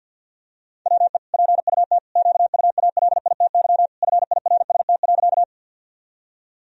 Morse code, heard as a keyed tone, about 52 wpm, 700 Hz. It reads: WE PUT QUALITY FIRST.
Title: Sentences from Top 1000 Words - Review Part 4 - 50wpm